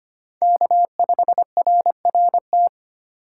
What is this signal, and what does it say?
Morse code, heard as a keyed tone, about 31 wpm, 700 Hz. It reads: K5RRT
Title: Speed Racing - USA 1x3 Call Signs 25wpm